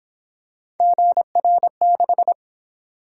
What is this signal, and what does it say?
Morse code, heard as a keyed tone, about 26 wpm, 700 Hz. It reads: GR6
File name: Speed-Racing - Three Letter-Number Combinations - 22wpm